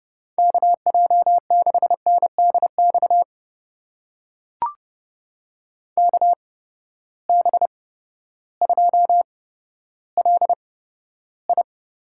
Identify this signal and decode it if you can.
Morse code, heard as a keyed tone, about 30 wpm, spaced as for 14 wpm, 700 Hz. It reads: KJ6NDX KB2LI
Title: USA Call Signs - 6 Characters - 8x Character Spacing - 30wpm